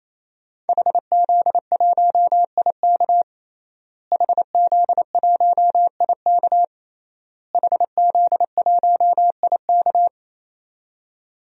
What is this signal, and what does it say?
Morse code, heard as a keyed tone, about 27 wpm, 700 Hz. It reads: HZ1IK HZ1IK HZ1IK
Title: DXCC-100 Call Signs - 5 Characters - Repeated 3x - 28wpm